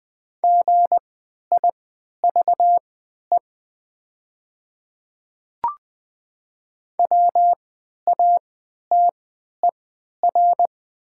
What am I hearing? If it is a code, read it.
Morse code, heard as a keyed tone, about 20 wpm, spaced as for 11 wpm, 700 Hz. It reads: GIVE WATER
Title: 3x Character Spacing - Top 300 Words - 20wpm